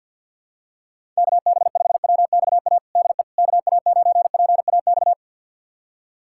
Morse code, heard as a keyed tone, about 47 wpm, 700 Hz. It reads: KB5WXA DE KA9PAX
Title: USA Callsign DE Callsign - 6 Characters - 50wpm